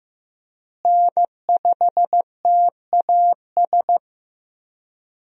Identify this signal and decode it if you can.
Morse code, heard as a keyed tone, about 15 wpm, 700 Hz. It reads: N5TAS
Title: USA Call Signs - 5 Characters - Repeated 3x - 15wpm